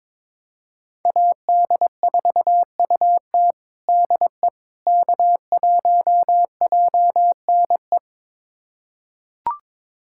Morse code, heard as a keyed tone, about 22 wpm, 700 Hz. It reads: AD4UT DE K1JNE
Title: USA Callsign DE Callsign - 5 Characters - 22wpm